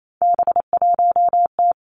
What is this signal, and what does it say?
Morse code, heard as a keyed tone, about 28 wpm, 690 Hz. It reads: B1T
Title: Three Challenging Letter-Number Combinations 28wpm